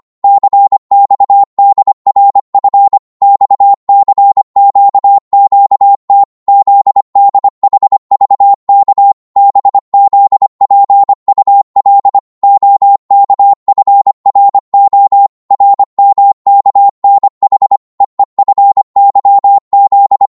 Morse code, heard as a keyed tone, about 25 wpm, 800 Hz. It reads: CXDRF XCQQT ZDHVK BZPUL OKFRO RMKNH EEFYZ